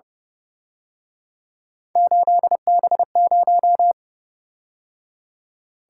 Morse code, heard as a keyed tone, about 29 wpm, 700 Hz. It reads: 8B0